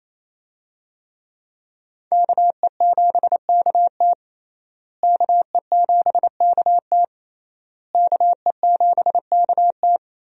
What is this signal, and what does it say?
Morse code, heard as a keyed tone, about 28 wpm, 700 Hz. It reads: KE7KT KE7KT KE7KT
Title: USA Call Signs - 5 Characters - Repeated 3x - 28wpm